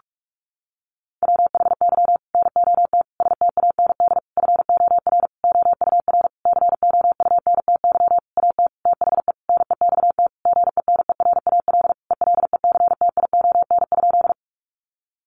Morse code, heard as a keyed tone, about 44 wpm, 700 Hz. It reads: WHY NOT STAND FOR OUR COUNTY AT THE NEXT GENERAL ELECTION?
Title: Sentences from Top 1500 Words 45wpm